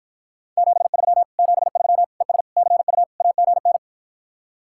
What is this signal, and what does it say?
Morse code, heard as a keyed tone, about 52 wpm, 700 Hz. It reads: 73 73 ES CU AGN